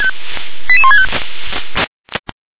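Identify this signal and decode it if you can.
Five-tone selective call (ZVEI), heard as quick runs of five tones, tones of about 70 ms. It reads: 88165
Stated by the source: ZVEI five-tone selective call received on SDR.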